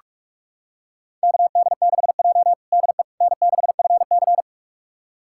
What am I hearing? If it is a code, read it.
Morse code, heard as a keyed tone, about 46 wpm, 700 Hz. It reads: KD6J DE N6FC
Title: USA Callsign DE Callsign - 4 Characters - 45wpm